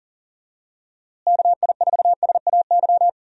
Morse code, heard as a keyed tone, about 40 wpm, 700 Hz. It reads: KI4SAY
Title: USA Call Signs - 6 Characters - Repeated 2x - 40wpm